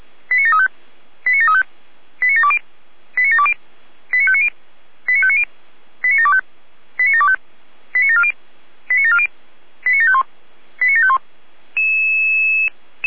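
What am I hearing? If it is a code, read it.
Five-tone selective call (ZVEI), heard as quick runs of five tones, tones of about 70 ms. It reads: 89735 89735 89720 89720 89590 89590 89725 89725 89740 89740 89751 89751
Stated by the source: ZVEI five-tone call received by SDR.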